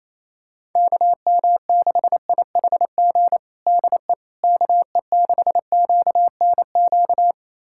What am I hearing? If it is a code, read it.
Morse code, heard as a keyed tone, about 28 wpm, 700 Hz. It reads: KM6IHG DE KE6QNQ
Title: USA Callsign DE Callsign - 6 Characters - 28wpm